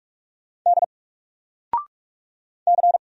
Morse code, heard as a keyed tone, about 46 wpm, 700 Hz. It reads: D C
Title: Single Letter-Number 45wpm